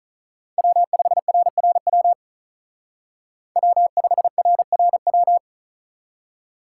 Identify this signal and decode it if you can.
Morse code, heard as a keyed tone, about 45 wpm, 700 Hz. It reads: W5RRW W5RRW